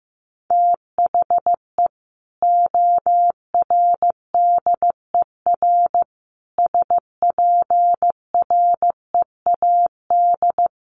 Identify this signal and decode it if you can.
Morse code, heard as a keyed tone, about 14 wpm, 700 Hz. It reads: THE ORDER SPREAD
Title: Sentences from Top 1200 Words 15wpm